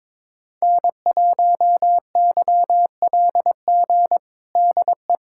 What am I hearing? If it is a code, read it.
Morse code, heard as a keyed tone, about 22 wpm, 700 Hz. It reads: N1YLG DE
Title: USA Callsign DE Callsign - 5 Characters - 22wpm